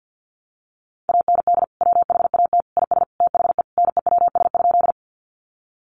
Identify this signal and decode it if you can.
Morse code, heard as a keyed tone, about 50 wpm, 700 Hz. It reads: AND WHAT IS THE NEWS?
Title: Sentences from Top 800 Words 50wpm